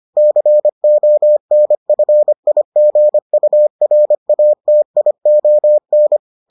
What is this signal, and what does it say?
Morse code, heard as a keyed tone, about 25 wpm, 600 Hz. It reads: CONFIGURATION